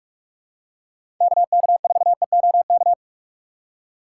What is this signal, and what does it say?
Morse code, heard as a keyed tone, about 44 wpm, 700 Hz. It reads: KK4EOK
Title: USA Call Signs - 6 Characters - 6x Character Spacing - 45wpm